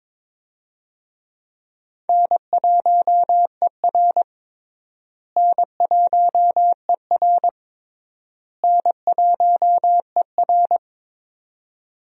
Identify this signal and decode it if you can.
Morse code, heard as a keyed tone, about 22 wpm, 700 Hz. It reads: N1ER N1ER N1ER
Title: USA Call Signs - 4 Characters - Repeated 3x - 22wpm